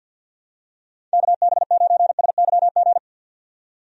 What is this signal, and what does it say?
Morse code, heard as a keyed tone, about 51 wpm, 700 Hz. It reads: KB9SYG
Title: USA Call Signs - 6 Characters - 50wpm